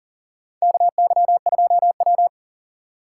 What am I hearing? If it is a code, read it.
Morse code, heard as a keyed tone, about 39 wpm, 700 Hz. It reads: KY2W
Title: USA Call Signs - 4 Characters - 4x Character Spacing - 40wpm